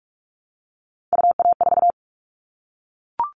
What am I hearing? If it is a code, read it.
Morse code, heard as a keyed tone, about 44 wpm, 700 Hz. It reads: UA4
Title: Callsign Prefixes for Top 100 DXCC - 45wpm